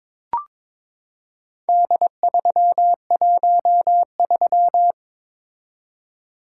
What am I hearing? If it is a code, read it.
Morse code, heard as a keyed tone, about 23 wpm, 690 Hz. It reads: D313
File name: practice / 22wpm / Four Letter-Number Combinations - 22wpm